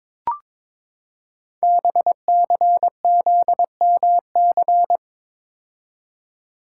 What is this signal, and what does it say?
Morse code, heard as a keyed tone, about 23 wpm, 700 Hz. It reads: BCZMC
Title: Code Groups - 5 Letters - 22wpm